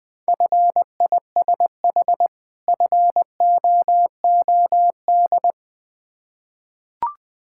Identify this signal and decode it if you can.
Morse code, heard as a keyed tone, about 21 wpm, 700 Hz. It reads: FISH FOOD
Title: Difficult Words with Hints - 20wpm